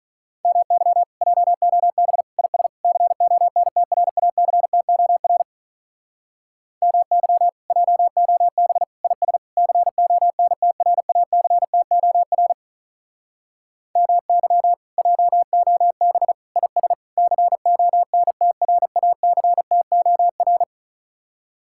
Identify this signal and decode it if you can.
Morse code, heard as a keyed tone, about 46 wpm, 700 Hz. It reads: MY JOB IS CONTRACTOR MY JOB IS CONTRACTOR MY JOB IS CONTRACTOR